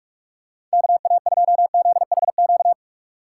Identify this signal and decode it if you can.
Morse code, heard as a keyed tone, about 46 wpm, 700 Hz. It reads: KA2ZHQ
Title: USA Call Signs - 6 Characters - 3x Character Spacing - 45wpm